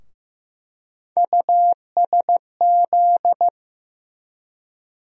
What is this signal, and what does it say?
Morse code, heard as a keyed tone, about 15 wpm, 700 Hz. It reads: USZ
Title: USA Call Sign Postfixes - 3 Letters 15wpm